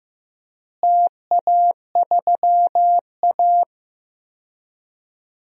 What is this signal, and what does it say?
Morse code, heard as a keyed tone, about 15 wpm, 700 Hz. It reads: TA3A
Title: DXCC-100 Call Signs - 4 Characters - Repeated 2x - 15wpm